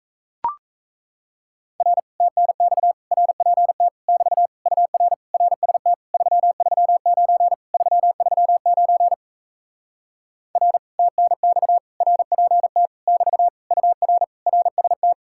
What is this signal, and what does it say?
Morse code, heard as a keyed tone, about 37 wpm, 700 Hz. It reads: R TNX RPT <BT> UR RST 339 339 R TNX RPT <BT> UR RST